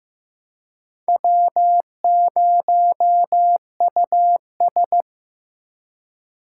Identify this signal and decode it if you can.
Morse code, heard as a keyed tone, about 18 wpm, 700 Hz. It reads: W0US
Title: Speed Racing - USA 1x2 Call Signs 15wpm